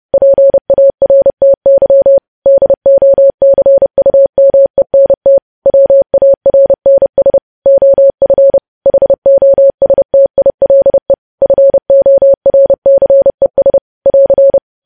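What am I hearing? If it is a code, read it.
Morse code, heard as a keyed tone, about 30 wpm, 550 Hz. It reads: PARTY DOCUMENT WARNS OF HOSTILE FORCES <AR>
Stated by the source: Morse news headlines at 30 wpm.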